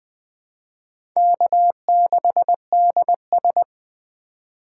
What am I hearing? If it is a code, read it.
Morse code, heard as a keyed tone, about 20 wpm, 700 Hz. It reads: K6DS
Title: USA Call Signs - 4 Characters - 3x Character Spacing - 20wpm